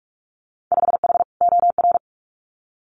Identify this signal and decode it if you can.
Morse code, heard as a keyed tone, about 46 wpm, 700 Hz. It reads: 5H OR
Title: Winter Field Day Exchanges Encore 45wpm